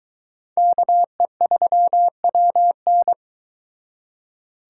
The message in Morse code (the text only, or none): KE3WN